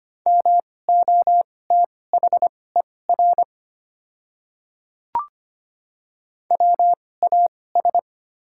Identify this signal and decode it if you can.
Morse code, heard as a keyed tone, about 25 wpm, spaced as for 18 wpm, 690 Hz. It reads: MOTHER WAS
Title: 2x Character Spacing - Top 300 Words - 25wpm